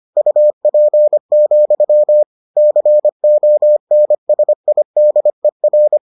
Morse code, heard as a keyed tone, about 25 wpm, 600 Hz. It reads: UP, CONSIDER